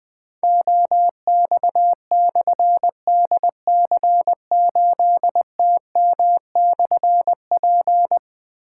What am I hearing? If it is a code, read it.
Morse code, heard as a keyed tone, about 20 wpm, 700 Hz. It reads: OX/DC8TM/P
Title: Difficult Call Signs 20wpm